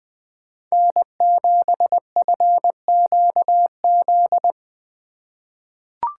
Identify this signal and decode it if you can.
Morse code, heard as a keyed tone, about 20 wpm, 700 Hz. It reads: N7FQZ
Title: USA Call Signs - 4x Character Spacing - 20wpm